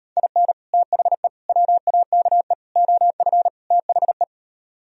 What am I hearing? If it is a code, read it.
Morse code, heard as a keyed tone, about 38 wpm, 700 Hz. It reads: IN THE WAKE OF THE